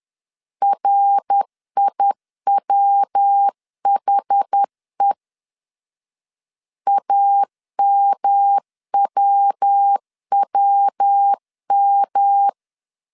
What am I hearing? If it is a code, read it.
Morse code, heard as a keyed tone, about 10 wpm, 800 Hz. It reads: RIWHE AMWWM
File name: CW / DET / 1 3 1500C S B 10wpm 3335